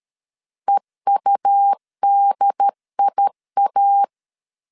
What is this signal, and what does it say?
Morse code, heard as a keyed tone, about 13 wpm, 800 Hz. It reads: EUDIA
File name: CW / DET / 1 3 1500C S B 12wpm 2828